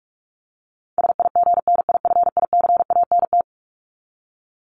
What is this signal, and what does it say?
Morse code, heard as a keyed tone, about 44 wpm, 700 Hz. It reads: SIGNIFICANT
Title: Top 751-1000 Words 45wpm